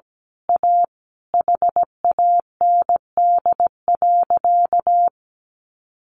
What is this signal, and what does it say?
Morse code, heard as a keyed tone, about 17 wpm, 700 Hz. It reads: A HAND.
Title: A Sound Of Thunder 17wpm